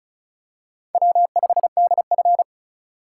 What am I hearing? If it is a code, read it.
Morse code, heard as a keyed tone, about 36 wpm, 700 Hz. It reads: W5DF
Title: USA Call Signs - 4 Characters - 3x Character Spacing - 35wpm